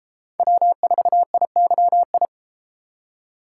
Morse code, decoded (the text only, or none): W4IYI